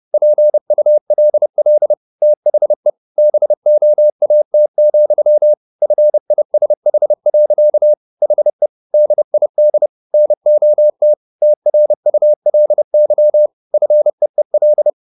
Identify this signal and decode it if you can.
Morse code, heard as a keyed tone, about 30 wpm, 600 Hz. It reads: PULL THE BOAT, FISH. HE DID NOT TRULY FEEL